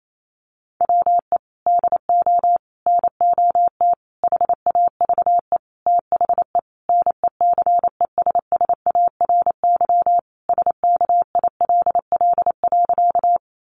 Morse code, heard as a keyed tone, about 28 wpm, 700 Hz. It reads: WE DO NOT HAVE THE NECESSARY SKILL.
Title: Sentences from Top 1200 Words 28wpm